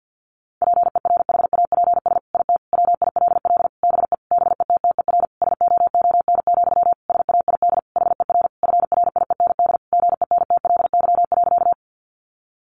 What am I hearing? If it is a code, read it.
Morse code, heard as a keyed tone, about 50 wpm, 700 Hz. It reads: PERHAPS IT WILL BE BETTER SOON, SAID HER FRIEND GENTLY.
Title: Sentences from Top 1500 Words 50wpm